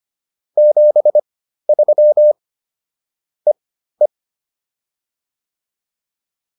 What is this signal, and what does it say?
Morse code, heard as a keyed tone, about 25 wpm, spaced as for 13 wpm, 600 Hz. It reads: 73 EE